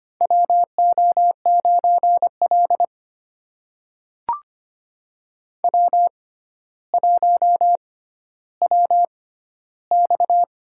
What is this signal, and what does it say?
Morse code, heard as a keyed tone, about 25 wpm, spaced as for 13 wpm, 700 Hz. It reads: WO9L W1WX
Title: USA Call Signs - 4 Characters - 6x Character Spacing - 25wpm